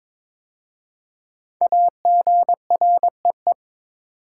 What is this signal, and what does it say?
Morse code, heard as a keyed tone, about 23 wpm, 700 Hz. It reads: AGREE
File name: 5 Letter Words - First 3 Letters Given - 22wpm